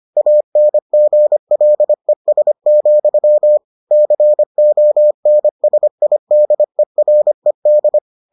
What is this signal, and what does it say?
Morse code, heard as a keyed tone, about 25 wpm, 600 Hz. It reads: ANGLES, CONSIDERED